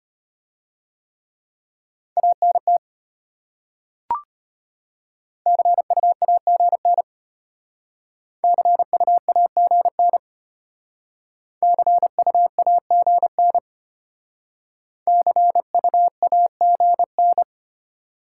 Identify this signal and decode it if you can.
Morse code, heard as a keyed tone, about 33 wpm, 700 Hz. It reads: ANT CUAGN CUAGN CUAGN CUAGN